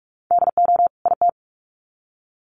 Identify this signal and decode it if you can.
Morse code, heard as a keyed tone, about 44 wpm, 700 Hz. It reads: DO IT
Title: Sets of 2 Words 45wpm